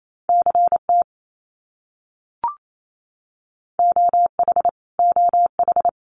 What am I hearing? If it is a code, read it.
Morse code, heard as a keyed tone, about 28 wpm, 700 Hz. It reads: CT OH OH
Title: US States Repeated 3x 28wpm